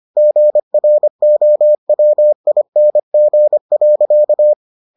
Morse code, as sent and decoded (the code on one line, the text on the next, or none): --. .-. --- .-- .. -. --. .-.-.-
GROWING.